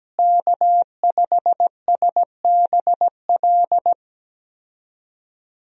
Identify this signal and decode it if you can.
Morse code, heard as a keyed tone, about 17 wpm, 700 Hz. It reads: K5SBL
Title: USA Call Signs - 5 Characters - Repeated 2x - 17wpm